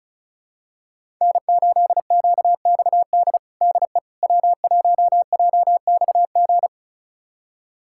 Morse code, heard as a keyed tone, about 34 wpm, 700 Hz. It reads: N8QXD DE W1JXG
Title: USA Callsign DE Callsign 35wpm